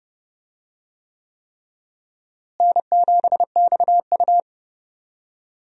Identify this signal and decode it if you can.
Morse code, heard as a keyed tone, about 32 wpm, 700 Hz. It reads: N7XU
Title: USA Call Signs - 4 Characters - Repeated 3x - 30wpm